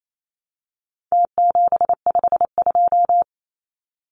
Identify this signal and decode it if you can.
Morse code, heard as a keyed tone, about 28 wpm, 700 Hz. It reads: T752
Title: Numbers - Contest Style - 4-Digit 28wpm